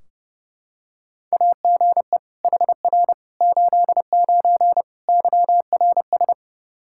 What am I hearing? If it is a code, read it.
Morse code, heard as a keyed tone, about 30 wpm, 700 Hz. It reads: AGE HR 89 YRS
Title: Practice QSO - Exchange 3 - More Info - 30wpm